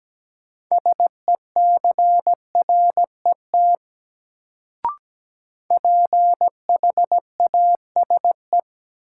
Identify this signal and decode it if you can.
Morse code, heard as a keyed tone, about 17 wpm, 700 Hz. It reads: SECRET PHASE
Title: Sentences from Top 1200 Words - New Word Drill - 17wpm